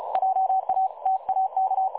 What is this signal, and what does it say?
Morse code, heard as a keyed tone, about 35 wpm, 760 Hz. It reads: JA TAK
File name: dlazaawansowanych / nagraniazpasma / websdr recording 3565.0kHz